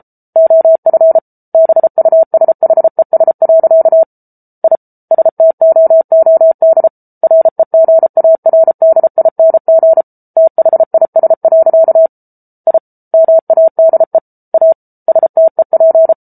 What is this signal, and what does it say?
Morse code, heard as a keyed tone, about 34 wpm, 650 Hz. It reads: OF BUSHES. I STOOD REGARDING THIS. I MADE A STEP